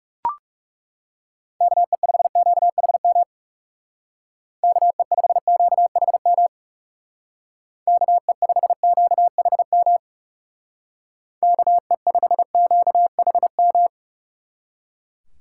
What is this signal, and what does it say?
Morse code, heard as a keyed tone, about 37 wpm, 700 Hz. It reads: KE5QHM KE5QHM KE5QHM KE5QHM
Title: Speed Racing - USA 2x3 Call Signs 30wpm